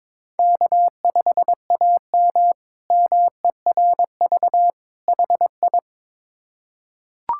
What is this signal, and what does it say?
Morse code, heard as a keyed tone, about 21 wpm, 700 Hz. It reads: K5AM MERV HI
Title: K1USN SST QSO Practice - 22wpm